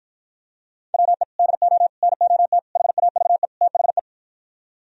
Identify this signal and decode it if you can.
Morse code, heard as a keyed tone, about 52 wpm, 700 Hz. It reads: WE DO NOT HAVE THE